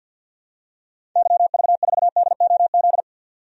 Morse code, heard as a keyed tone, about 51 wpm, 700 Hz. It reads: YV4DOZ
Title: DXCC-100 Call Signs - 6 Characters - 50wpm